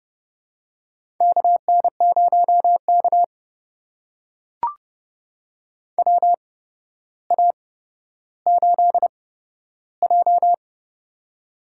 Morse code, heard as a keyed tone, about 30 wpm, spaced as for 13 wpm, 700 Hz. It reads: KN0K WA8J
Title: USA Call Signs - 4 Characters - 8x Character Spacing - 30wpm